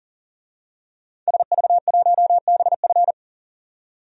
Morse code, heard as a keyed tone, about 40 wpm, 700 Hz. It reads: SV1BF
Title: DXCC-100 Call Signs - 5 Characters - 40wpm